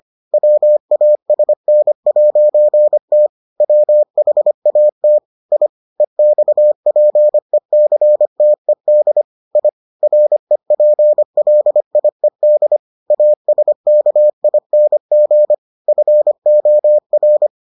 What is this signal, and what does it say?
Morse code, heard as a keyed tone, about 25 wpm, 600 Hz. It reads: WASN'T WHAT I EXPECTED I REPLIED ASKING FOR